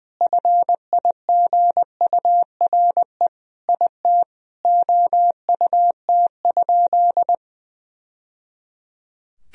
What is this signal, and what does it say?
Morse code, heard as a keyed tone, about 20 wpm, 700 Hz. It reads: FIGURE IT OUT?
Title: Sentences from Top 1000 Words - Review Part 2 - 20wpm